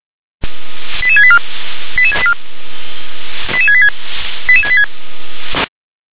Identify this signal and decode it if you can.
Five-tone selective call (ZVEI), heard as quick runs of five tones, tones of about 70 ms. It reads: 88684 88686 88686
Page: /357/SDR/ZVEI/88770/